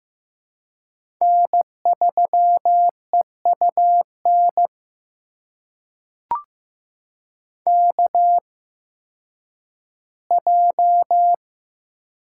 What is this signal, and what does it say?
Morse code, heard as a keyed tone, about 15 wpm, spaced as for 10 wpm, 700 Hz. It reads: N3EUN KJ
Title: USA Call Signs - 5 Characters - 8x Character Spacing - 15wpm